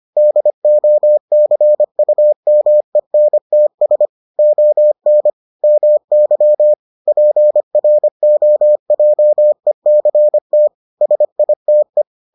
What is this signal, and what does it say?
Morse code, heard as a keyed tone, about 25 wpm, 600 Hz. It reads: DOCUMENTS ON MY PROJECT SITE